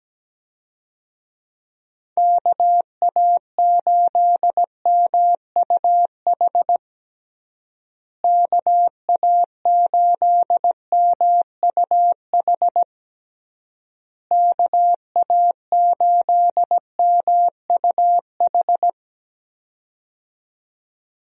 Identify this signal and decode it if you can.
Morse code, heard as a keyed tone, about 17 wpm, 700 Hz. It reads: KA8MUH KA8MUH KA8MUH
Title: USA Call Signs - 6 Characters - Repeated 3x - 17wpm